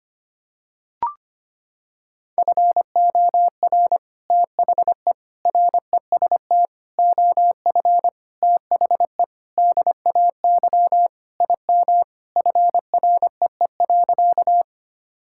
Morse code, heard as a keyed tone, about 25 wpm, 700 Hz. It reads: FOR THE REST OF THE DAY IM FREE.
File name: practice / 25wpm / Sentences with 3-Word Phrases - 25wpm